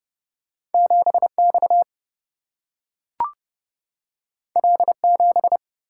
Morse code, heard as a keyed tone, about 30 wpm, 700 Hz. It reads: 7 X L 7
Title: Single-Letter Single-Number 30wpm